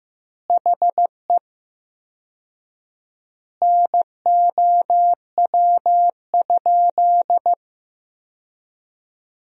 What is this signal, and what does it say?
Morse code, heard as a keyed tone, about 15 wpm, 700 Hz. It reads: HE NOW?
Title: 4x Word-Spacing - Sentences from Top 200 Words - 15wpm